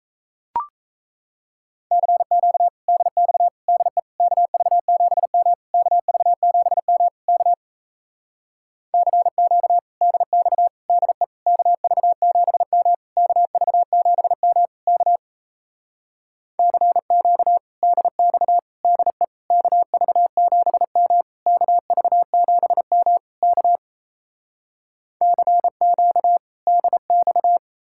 Morse code, heard as a keyed tone, about 37 wpm, 700 Hz. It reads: CQ DX DE KV7M KV7M K CQ DX DE KV7M KV7M K CQ DX DE KV7M KV7M K CQ DX